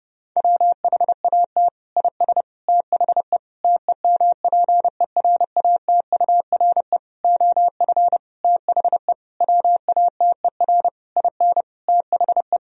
Morse code, heard as a keyed tone, about 30 wpm, 700 Hz. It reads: WHAT IS THE TEMPERATURE OF THE WATER IN THE
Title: Sentences from Top 1000 Words - Review Part 4 - 30wpm